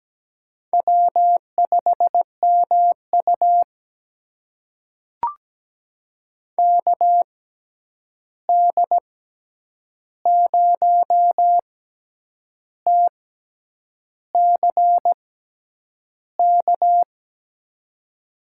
Morse code, heard as a keyed tone, about 17 wpm, spaced as for 8 wpm, 700 Hz. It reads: W5MU KD0TCK
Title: USA Call Signs - 6x Character Spacing - 17wpm